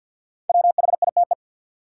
Morse code, heard as a keyed tone, about 48 wpm, 700 Hz. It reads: WHITE
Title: Sets of 2 Words - New Word Drill - 50wpm